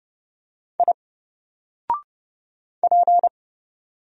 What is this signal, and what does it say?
Morse code, heard as a keyed tone, about 30 wpm, 700 Hz. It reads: I P
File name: Single-Letter-Number-30wpm